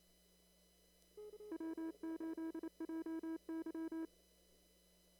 Morse code, heard as a keyed tone, about 28 wpm, 320 Hz. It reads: W8JY
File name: CW / kt5j wpx cw-op-k5pi